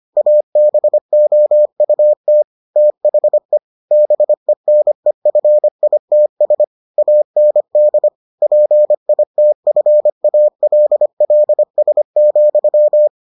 Morse code, heard as a keyed tone, about 25 wpm, 600 Hz. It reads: ABOUT THE BENEFITS AND PITFALLS,